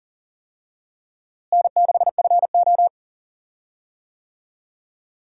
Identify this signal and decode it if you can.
Morse code, heard as a keyed tone, about 40 wpm, 700 Hz. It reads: N6FO
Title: USA Call Signs - 4 Characters - Repeated 2x - 40wpm